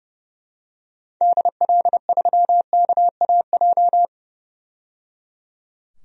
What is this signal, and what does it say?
Morse code, heard as a keyed tone, about 30 wpm, 700 Hz. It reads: DL3KAJ
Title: DXCC-100 Call Signs - 6 Characters - 30wpm